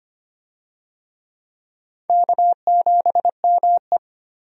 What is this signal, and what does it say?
Morse code, heard as a keyed tone, about 25 wpm, 700 Hz. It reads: K7ME
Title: USA Call Signs - 4 Characters - 25wpm